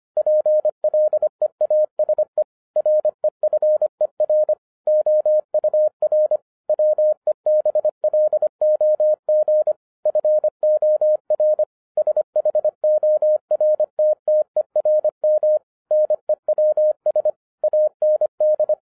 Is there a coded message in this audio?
Yes: Morse code from 0.2 to 18.7 s, heard as a keyed tone, about 25 wpm, 610 Hz. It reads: PLEASE REFER OUR WEBLOG FOR SHORTTERM NEWS AND